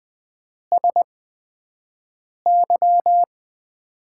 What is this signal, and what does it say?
Morse code, heard as a keyed tone, about 20 wpm, spaced as for 5 wpm, 700 Hz. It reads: SY
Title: USA Call Signs - 6 Characters - 8x Character Spacing - 20wpm